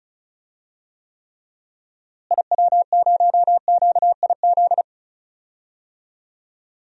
Morse code, heard as a keyed tone, about 36 wpm, 700 Hz. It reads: IW0QIZ
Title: DXCC-100 Call Signs - 6 Characters - Repeated 3x - 35wpm